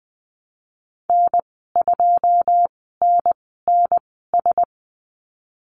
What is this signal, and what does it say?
Morse code, heard as a keyed tone, about 20 wpm, spaced as for 15 wpm, 700 Hz. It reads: N2NNS
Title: 140 Lesson 23g - Letter f - Call Signs 20wpm